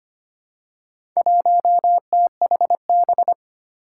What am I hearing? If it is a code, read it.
Morse code, heard as a keyed tone, about 25 wpm, 700 Hz. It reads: 1THB
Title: Four Challenging Letter-Number Combinations 25wpm